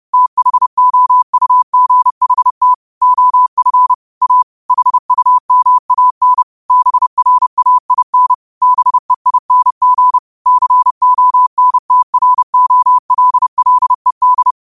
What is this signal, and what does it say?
Morse code, heard as a keyed tone, about 30 wpm, 1000 Hz. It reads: THOUGHT OF A HUMAN BRAIN BEING CONTROLLED